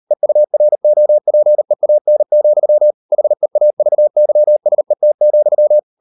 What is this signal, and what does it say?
Morse code, heard as a keyed tone, about 38 wpm, 600 Hz. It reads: EUROPEAN, HEAVYSET,